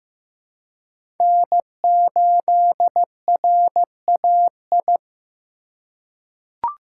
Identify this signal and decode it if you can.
Morse code, heard as a keyed tone, about 15 wpm, 700 Hz. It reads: N8RAI
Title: USA Call Signs - 5 Characters - 2x Character Spacing - 15wpm